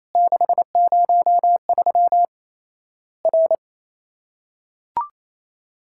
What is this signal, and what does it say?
Morse code, heard as a keyed tone, about 28 wpm, 700 Hz. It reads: 603 R
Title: Worked All Europe DX Contest - QTC Exchange - 28wpm